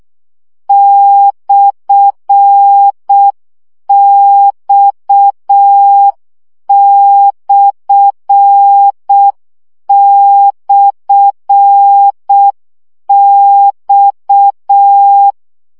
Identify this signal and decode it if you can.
Morse code, heard as a keyed tone, about 6 wpm, 800 Hz. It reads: /X//X
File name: CW gyakorlas x per 5 baud